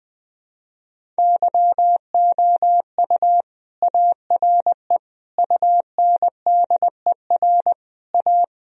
Morse code, heard as a keyed tone, about 20 wpm, 700 Hz. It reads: YOU ARE UNDER A
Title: Sentences from Top 1000 Words - Review Part 1 - 20wpm